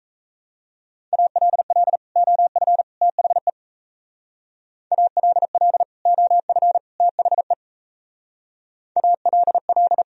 Morse code, heard as a keyed tone, about 34 wpm, 700 Hz. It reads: ALL OF THE ALL OF THE ALL
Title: Speed-Racing - Sets of 3 Words 28wpm